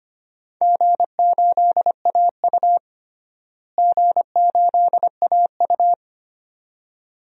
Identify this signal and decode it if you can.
Morse code, heard as a keyed tone, about 25 wpm, 700 Hz. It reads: G8AU G8AU
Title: DXCC-100 Call Signs - 4 Characters - Repeated 2x - 25wpm